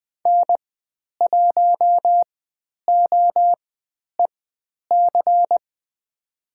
Morse code, heard as a keyed tone, about 20 wpm, spaced as for 10 wpm, 700 Hz. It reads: N1OEC
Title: 109 Lesson 18g - Letter m - Call Signs 20wpm